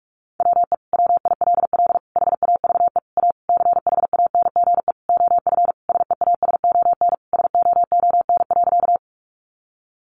Morse code, heard as a keyed tone, about 44 wpm, 700 Hz. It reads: WE WILL HAVE A CHANGE OF SEASON SOON.
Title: Sentences from Top 1400 Words 45wpm